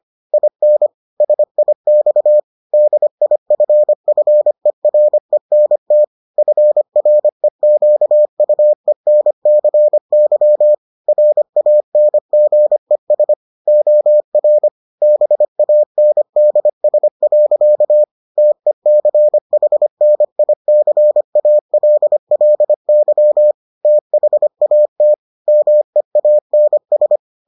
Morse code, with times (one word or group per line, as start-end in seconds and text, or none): 0.3-0.9 IN
1.2-2.4 SIX
2.7-6.0 DIFFERENT
6.4-10.7 FREQUENCY
11.1-13.3 RANGES
13.7-14.7 OR
15.0-18.0 BANDS.
18.4-23.5 TECHNICALLY
23.9-25.1 THAT
25.5-27.2 MEANS